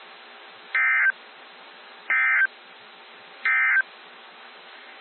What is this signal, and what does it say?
Emergency Alert System (SAME) tones, heard as harsh buzzing bursts. It is heard from 0.6 to 3.9 s.
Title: NOAA Weather Radio - WXJ73